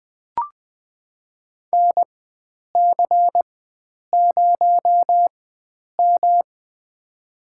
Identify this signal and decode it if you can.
Morse code, heard as a keyed tone, about 21 wpm, spaced as for 9 wpm, 700 Hz. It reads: NC0M
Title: USA Call Signs - 4 Characters - 4x Character Spacing - 20wpm